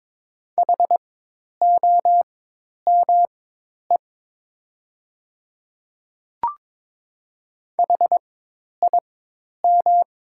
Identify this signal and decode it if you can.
Morse code, heard as a keyed tone, about 22 wpm, spaced as for 10 wpm, 700 Hz. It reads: HOME HIM
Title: 4x Character Spacing - Top 300 Words - 22wpm